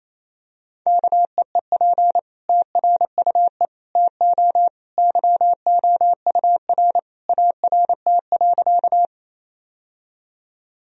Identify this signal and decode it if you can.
Morse code, heard as a keyed tone, about 28 wpm, 700 Hz. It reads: KEEP TRUE TO YOUR ART.
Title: Sentences from Top 1000 Words - Review Part 3 - 28wpm